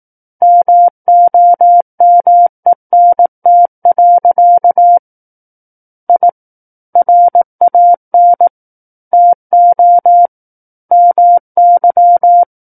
Morse code, heard as a keyed tone, about 18 wpm, 700 Hz. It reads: MOMENT. I RAN TO MY
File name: 18 WPM